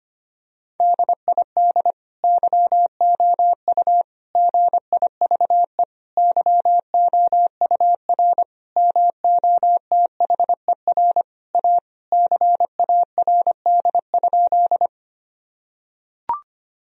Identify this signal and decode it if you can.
Morse code, heard as a keyed tone, about 25 wpm, 700 Hz. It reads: DID YOU GIVE YOUR MOTHER A CARD?
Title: Sentences from Top 1000 Words - Review Part 4 - 25wpm